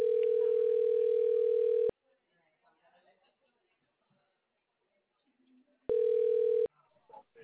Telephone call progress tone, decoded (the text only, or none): ringback tone